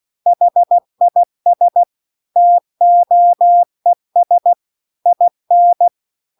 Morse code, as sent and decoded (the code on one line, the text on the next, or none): .... .. ... / - --- . ... / .. -.
HIS TOES IN